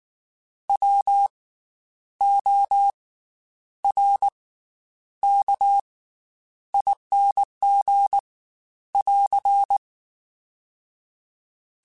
Morse code, heard as a keyed tone, about 19 wpm, spaced as for 9 wpm, 800 Hz. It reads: WORKING <AR>
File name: Special INGs / Special INGs-mp3s / Slow ing @18